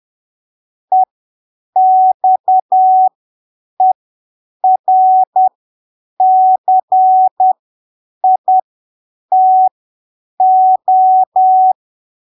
Morse code, heard as a keyed tone, about 10 wpm, spaced as for 7 wpm, 750 Hz. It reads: EXERCITO
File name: arquivos / texto / texto 14 10ppm